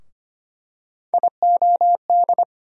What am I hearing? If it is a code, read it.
Morse code, heard as a keyed tone, about 25 wpm, 700 Hz. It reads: IOD
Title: USA Call Sign Postfixes - 3 Letters 25wpm